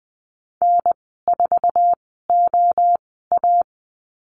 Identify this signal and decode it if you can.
Morse code, heard as a keyed tone, about 20 wpm, spaced as for 14 wpm, 700 Hz. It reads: N4OA
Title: 048 Lesson 9g - Number 4 - Call Signs 20wpm